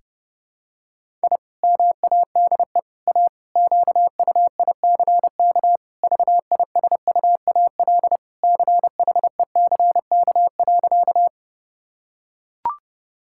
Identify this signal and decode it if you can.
Morse code, heard as a keyed tone, about 30 wpm, 700 Hz. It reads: I MADE A QUICK VISUAL CHECK.
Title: Sentences from Top 1000 Words - Review Part 6 - 30wpm